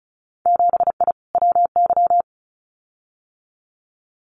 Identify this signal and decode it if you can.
Morse code, heard as a keyed tone, about 34 wpm, 700 Hz. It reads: 7I WY
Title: Winter Field Day Exchanges Encore 35wpm